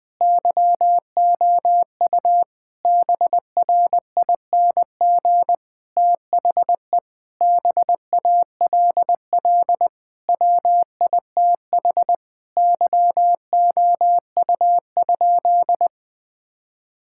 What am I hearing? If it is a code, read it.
Morse code, heard as a keyed tone, about 21 wpm, 700 Hz. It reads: YOU BRING THE BALL WITH YOU?
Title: Sentences from Top 1000 Words - Review Part 6 - 20wpm